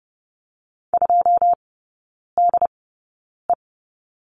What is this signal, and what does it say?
Morse code, heard as a keyed tone, about 30 wpm, spaced as for 9 wpm, 700 Hz. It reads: 2DE
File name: 081 Lesson 14e - Number 2 - Three Characters 30wpm